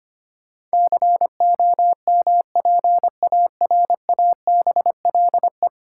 Morse code, heard as a keyed tone, about 25 wpm, 700 Hz. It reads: COMPARABLE
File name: 10 Letter Words - First 3 Letters Given - 25wpm